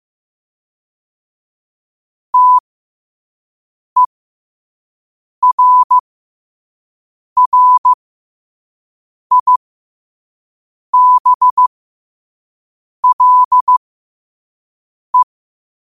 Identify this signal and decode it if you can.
Morse code, heard as a keyed tone, about 16 wpm, spaced as for 5 wpm, 1000 Hz. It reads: TERRIBLE
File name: qotdfiles / 5wpm